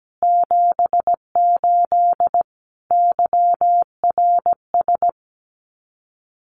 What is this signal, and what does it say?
Morse code, heard as a keyed tone, about 17 wpm, 700 Hz. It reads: 78 YRS
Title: Common QSO Phrases Encore 17wpm